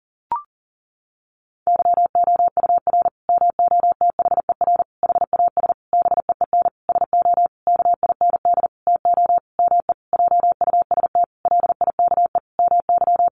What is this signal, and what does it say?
Morse code, heard as a keyed tone, about 40 wpm, 700 Hz. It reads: YOUR MOTHER HAS BEEN SO KIND TO ME JUST LIKE MY
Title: Sentences from Top 1200 Words 40wpm